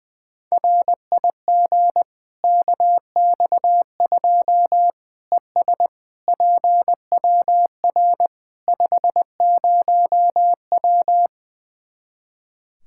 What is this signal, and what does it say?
Morse code, heard as a keyed tone, about 21 wpm, 700 Hz. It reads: RIG KX2 ES PWR 50W
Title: Practice QSO - Exchange 2 - Info - 20wpm